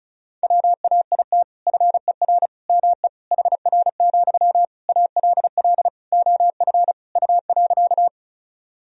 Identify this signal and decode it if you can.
Morse code, heard as a keyed tone, about 36 wpm, 700 Hz. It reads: WAIT FER ME HR, ALL OF U.
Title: Sentences from Top 500 Words and Abbreviations 35wpm